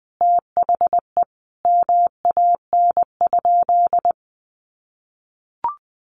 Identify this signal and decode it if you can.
Morse code, heard as a keyed tone, about 20 wpm, 700 Hz. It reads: THE MAN?
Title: Sentences from Top 1300 Words 20wpm